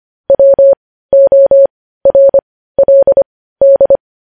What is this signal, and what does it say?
Morse code, heard as a keyed tone, about 25 wpm, spaced as for 15 wpm, 550 Hz. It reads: WORLD
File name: podcast / News headlines-1x-25x15